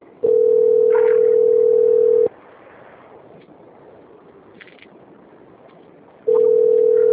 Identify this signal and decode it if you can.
Ringback tone (North America), 440 + 480 Hz.
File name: out- - - - - .52219